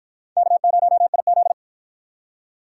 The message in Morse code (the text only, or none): K0IZ